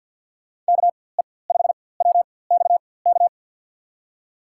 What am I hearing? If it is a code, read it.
Morse code, heard as a keyed tone, about 53 wpm, spaced as for 24 wpm, 700 Hz. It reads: KE5WXK
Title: USA Call Signs - 4x Character Spacing - 50wpm